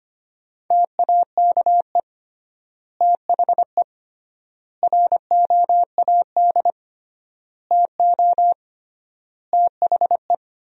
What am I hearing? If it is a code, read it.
Morse code, heard as a keyed tone, about 25 wpm, 700 Hz. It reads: TAKE THE ROAD TO THE
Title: 3x Word-Spacing - Sentences from Top 400 Words - 25wpm